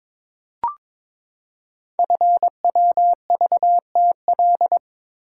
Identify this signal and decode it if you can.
Morse code, heard as a keyed tone, about 22 wpm, 700 Hz. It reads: FWVTL
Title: Code Groups - 5 Letter:Numbers - 22wpm